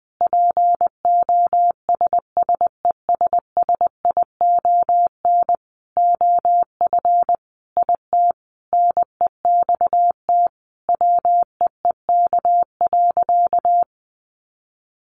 Morse code, heard as a keyed tone, about 20 wpm, 700 Hz. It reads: POSSESSION OF IT NEXT WEEK.
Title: Sentences from Top 1400 Words 20wpm